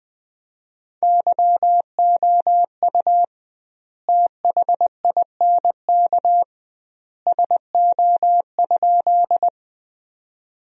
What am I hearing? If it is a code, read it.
Morse code, heard as a keyed tone, about 20 wpm, 700 Hz. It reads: YOU THINK SO?
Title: 2x Word-Spacing - Sentences from Top 200 Words - 20wpm